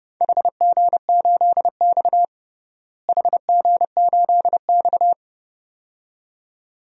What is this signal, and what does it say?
Morse code, heard as a keyed tone, about 29 wpm, 700 Hz. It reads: HG8X HG8X
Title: DXCC-100 Call Signs - 4 Characters - Repeated 2x - 30wpm